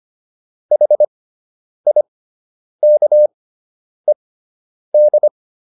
Morse code, heard as a keyed tone, about 25 wpm, spaced as for 9 wpm, 600 Hz. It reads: HIKED
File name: ED5 09